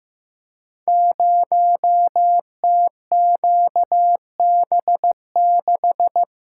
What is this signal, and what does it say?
Morse code, heard as a keyed tone, about 15 wpm, 700 Hz. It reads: 0TQB6
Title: Code Groups - 5 Letter:Numbers - 15wpm